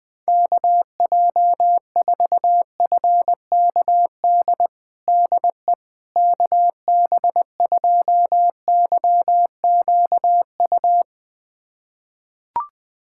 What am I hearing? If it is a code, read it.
Morse code, heard as a keyed tone, about 21 wpm, 690 Hz. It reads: KJ4FKD DE KB2YQU
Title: USA Callsign DE Callsign 20wpm